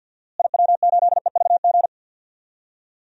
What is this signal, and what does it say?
Morse code, heard as a keyed tone, about 46 wpm, 700 Hz. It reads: IW8EVG
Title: DXCC-100 Call Signs - 6 Characters - Repeated 2x - 50wpm